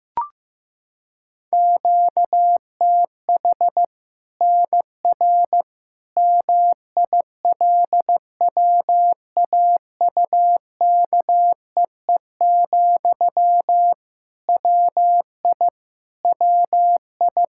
Morse code, heard as a keyed tone, about 15 wpm, 700 Hz. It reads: QTH NR MILWAUKEE, WI WI
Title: K1USN SST QSO Practice 15wpm